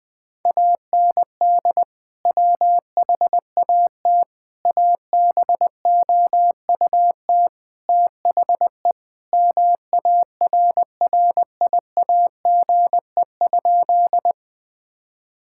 Morse code, heard as a keyed tone, about 21 wpm, 700 Hz. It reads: AND WHAT ABOUT THE MARRIAGE?
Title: Sentences from Top 1000 Words - Review Part 5 - 20wpm